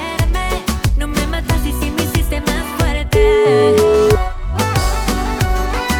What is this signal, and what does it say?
Ringback tone (Europe, 425 Hz), with noise.